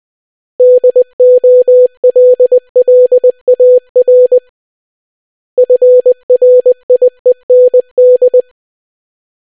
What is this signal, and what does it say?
Morse code, heard as a keyed tone, about 20 wpm, 500 Hz. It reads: DOLLAR FRIEND